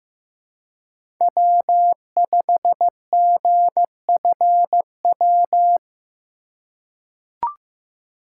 Morse code, heard as a keyed tone, about 15 wpm, 700 Hz. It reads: W5GFW
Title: USA Call Signs - 5 Characters - 2x Character Spacing - 15wpm